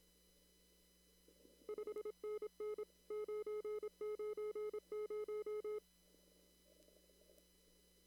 Morse code, heard as a keyed tone, about 26 wpm, 1270 Hz. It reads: 5NN 990
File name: CW / kt5j wpx cw-op-k5pi